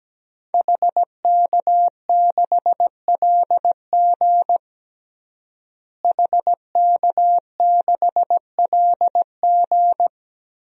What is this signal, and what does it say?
Morse code, heard as a keyed tone, about 17 wpm, 700 Hz. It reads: HK6LG HK6LG
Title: DXCC-100 Call Signs - 5 Characters - Repeated 3x - 17wpm